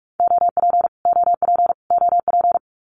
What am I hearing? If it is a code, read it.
Morse code, heard as a keyed tone, about 44 wpm, 700 Hz. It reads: OP OP OP
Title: Abbreviations Repeated 3x - 45wpm